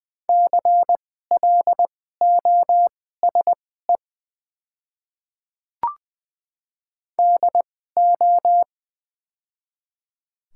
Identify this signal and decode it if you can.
Morse code, heard as a keyed tone, about 21 wpm, spaced as for 15 wpm, 700 Hz. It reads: CLOSE DO
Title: 2x Character Spacing - Top 300 Words - 20wpm